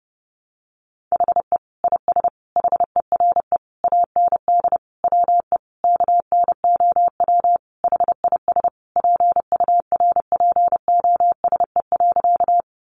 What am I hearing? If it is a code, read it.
Morse code, heard as a keyed tone, about 30 wpm, 700 Hz. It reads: HE IS HERE AND WE KNOW HIS PURPOSE.
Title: Sentences from Top 800 Words 30wpm